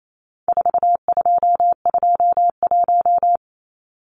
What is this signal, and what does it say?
Morse code, heard as a keyed tone, about 28 wpm, 700 Hz. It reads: 4221
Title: Numbers - 4-Digit 28wpm